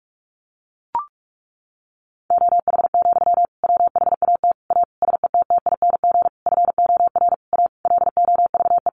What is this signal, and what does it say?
Morse code, heard as a keyed tone, about 44 wpm, 700 Hz. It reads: OH, WHAT A SETTING FOR A LOVE
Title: Sentences from Top 1300 Words 45wpm